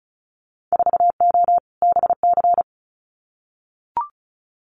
Morse code, heard as a keyed tone, about 34 wpm, 700 Hz. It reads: 4O BC
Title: Winter Field Day Exchanges Encore 35wpm